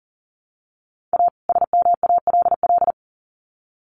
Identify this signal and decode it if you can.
Morse code, heard as a keyed tone, about 40 wpm, 700 Hz. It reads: A SMALL
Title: Sets of 2 Words 40wpm